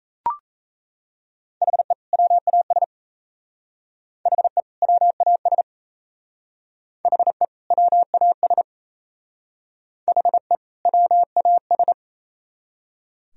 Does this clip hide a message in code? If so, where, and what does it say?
Yes: Morse code from 1.6 to 11.9 s, heard as a keyed tone, about 37 wpm, 700 Hz. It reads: HE WAS HE WAS HE WAS HE WAS